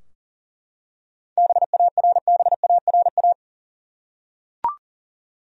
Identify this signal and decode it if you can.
Morse code, heard as a keyed tone, about 40 wpm, 700 Hz. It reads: BARBARA
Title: USA Common Names - 40wpm